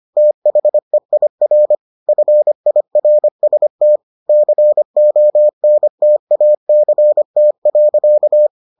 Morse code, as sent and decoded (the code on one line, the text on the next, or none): - .... .. .-. / ..-. .. .-. ... - / -.-. --- -. - .- -.-. - .-.-.-
THEIR FIRST CONTACT.